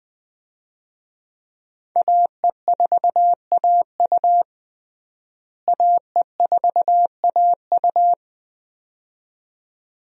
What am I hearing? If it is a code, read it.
Morse code, heard as a keyed tone, about 20 wpm, 700 Hz. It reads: AE4AU AE4AU
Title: USA Call Signs - 5 Characters - Repeated 2x - 20wpm